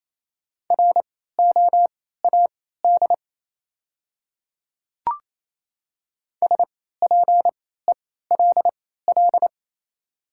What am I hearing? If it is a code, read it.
Morse code, heard as a keyed tone, about 28 wpm, spaced as for 16 wpm, 700 Hz. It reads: ROAD SPELL
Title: 3x Character Spacing - Top 500 Words - 28wpm